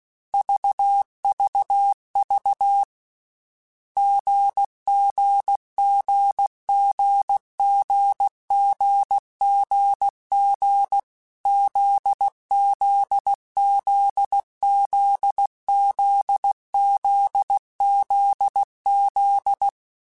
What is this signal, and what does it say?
Morse code, heard as a keyed tone, about 16 wpm, 800 Hz. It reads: VVV GGGGGGGG ZZZZZZZZ
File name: Difficult Combos / Difficult Combos-mp3s / GZ GZ7 @15